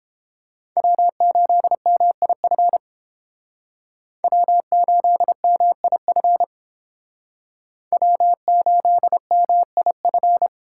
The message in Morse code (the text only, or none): W8MIF W8MIF W8MIF